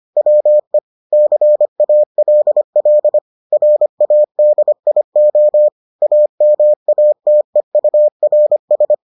Morse code, heard as a keyed tone, about 25 wpm, 600 Hz. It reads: WE CALL RADIO AMATEURS